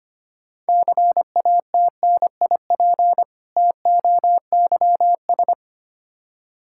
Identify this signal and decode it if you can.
Morse code, heard as a keyed tone, about 25 wpm, 700 Hz. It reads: CATNIP TOYS
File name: practice / 25wpm / Difficult Words with Hints - 25wpm